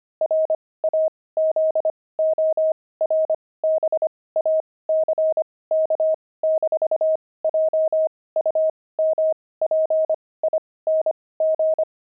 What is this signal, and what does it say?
Morse code, heard as a keyed tone, about 25 wpm, spaced as for 18 wpm, 630 Hz. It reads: RAZORBACK-JUMPING
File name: pangram-fs-18-cs-25-frogs